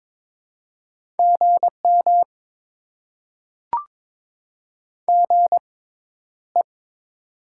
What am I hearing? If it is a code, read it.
Morse code, heard as a keyed tone, about 21 wpm, spaced as for 11 wpm, 700 Hz. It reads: GM GE
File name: Top 20 QSO Elements - 6x Character Spacing - 22wpm